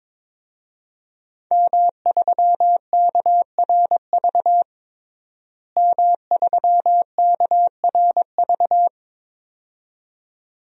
Morse code, heard as a keyed tone, about 22 wpm, 700 Hz. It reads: M3KRV M3KRV
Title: DXCC-100 Call Signs - 5 Characters - Repeated 2x - 22wpm